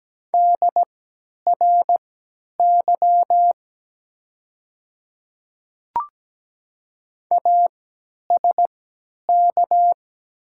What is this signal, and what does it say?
Morse code, heard as a keyed tone, about 17 wpm, spaced as for 10 wpm, 700 Hz. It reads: DRY ASK
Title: 3x Character Spacing - Top 500 Words - 17wpm